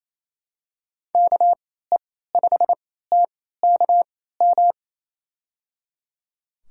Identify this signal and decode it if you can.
Morse code, heard as a keyed tone, about 28 wpm, spaced as for 16 wpm, 700 Hz. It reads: KE5TKM
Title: USA Call Signs - 6 Characters - 3x Character Spacing - 28wpm